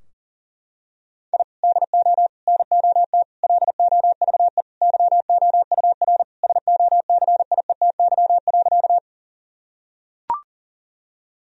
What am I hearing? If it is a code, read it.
Morse code, heard as a keyed tone, about 40 wpm, 700 Hz. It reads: I DO NOT LOVE YOUR SOCIETY.
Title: Sentences from Top 1000 Words - Review Part 4 - 40wpm